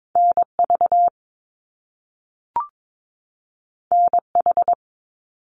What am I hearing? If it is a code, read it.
Morse code, heard as a keyed tone, about 22 wpm, 700 Hz. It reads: NV NH
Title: US States 22wpm